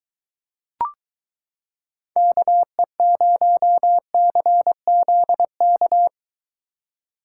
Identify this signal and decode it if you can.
Morse code, heard as a keyed tone, about 23 wpm, 700 Hz. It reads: KE0CZK